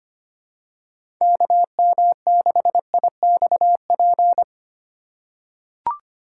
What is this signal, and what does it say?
Morse code, heard as a keyed tone, about 25 wpm, 700 Hz. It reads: KM6IXP
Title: USA Call Signs - 6 Characters - 3x Character Spacing - 25wpm